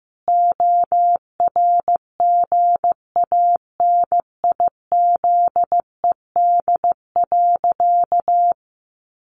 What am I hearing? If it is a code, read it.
Morse code, heard as a keyed tone, about 15 wpm, 700 Hz. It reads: ORGANIZED.
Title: Sentences from Top 1300 Words 15wpm